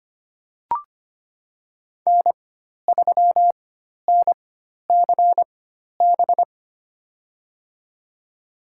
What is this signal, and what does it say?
Morse code, heard as a keyed tone, about 25 wpm, spaced as for 12 wpm, 700 Hz. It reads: N3NCB
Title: USA Call Signs - 4x Character Spacing - 25wpm